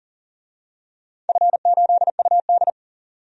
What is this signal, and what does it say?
Morse code, heard as a keyed tone, about 41 wpm, 700 Hz. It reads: F8UD
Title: DXCC-100 Call Signs - 4 Characters - 40wpm